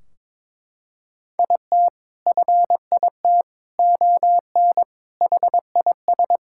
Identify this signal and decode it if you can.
Morse code, heard as a keyed tone, about 22 wpm, 700 Hz. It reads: IT FIT ON HIS
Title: Sentences from Top 1600 Words 22wpm